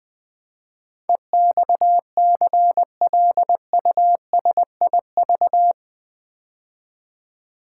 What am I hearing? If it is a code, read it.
Morse code, heard as a keyed tone, about 20 wpm, 700 Hz. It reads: EXCLUSIV
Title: Build A Word - 4 Letters At A Time - 11-Letter Words - 20wpm